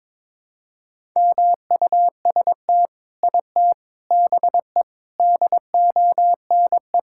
Morse code, heard as a keyed tone, about 22 wpm, 700 Hz. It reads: MUST IT BE DONE
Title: Sentences from Top 1000 Words - Review Part 5 - 22wpm